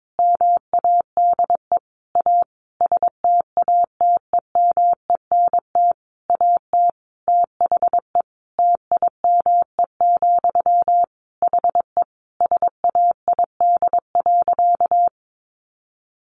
Morse code, heard as a keyed tone, about 22 wpm, 700 Hz. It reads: MADE A STATEMENT AT THE TIME, HE SAID.
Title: Sentences from Top 1200 Words 22wpm